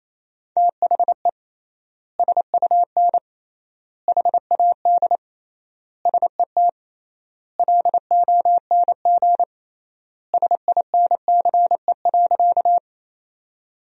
Morse code, heard as a keyed tone, about 28 wpm, 700 Hz. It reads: THE SUN HAD SET LONG SINCE.
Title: 3x Word-Spacing - Sentences from Top 300 Words - 28wpm